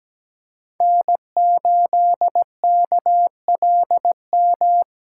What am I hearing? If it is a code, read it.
Morse code, heard as a keyed tone, about 17 wpm, 700 Hz. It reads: N8KLM
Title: USA Call Signs - 5 Characters - 3x Character Spacing - 17wpm